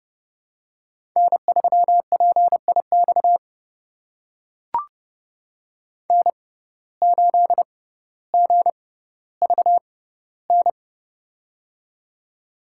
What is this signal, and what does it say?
Morse code, heard as a keyed tone, about 30 wpm, spaced as for 15 wpm, 700 Hz. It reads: N3PIX N8GVN
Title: USA Call Signs - 5 Characters - 6x Character Spacing - 30wpm